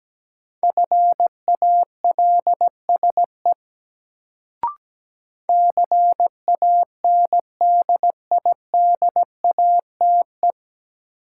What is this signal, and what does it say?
Morse code, heard as a keyed tone, about 17 wpm, 700 Hz. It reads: FALSE CANDIDATE
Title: Sentences from Top 1600 Words - New Word Drill - 17wpm